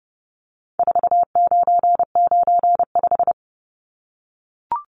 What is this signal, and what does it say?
Morse code, heard as a keyed tone, about 30 wpm, 700 Hz. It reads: 4995
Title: Numbers - 4-Digit 30wpm